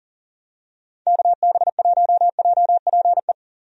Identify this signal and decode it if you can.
Morse code, heard as a keyed tone, about 40 wpm, 700 Hz. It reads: KB1JPE
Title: USA Call Signs - 6 Characters - Repeated 3x - 40wpm